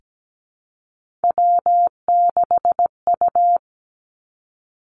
Morse code, heard as a keyed tone, about 17 wpm, 700 Hz. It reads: W6U
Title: Three Challenging Letter-Number Combinations 17wpm